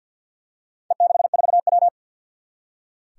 Morse code, heard as a keyed tone, about 46 wpm, 700 Hz. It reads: E64W
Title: Four Challenging Letter-Number Combinations 50wpm